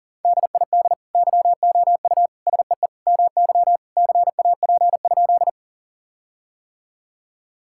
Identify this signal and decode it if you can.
Morse code, heard as a keyed tone, about 40 wpm, 700 Hz. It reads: DID YOU SEE MY CAP?
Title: Sentences from Top 1600 Words 40wpm